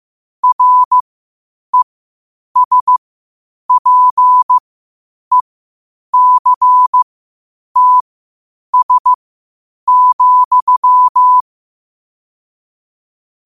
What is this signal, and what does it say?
Morse code, heard as a keyed tone, about 15 wpm, spaced as for 8 wpm, 1000 Hz. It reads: RESPECTS,
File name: qotdfiles / 8wpm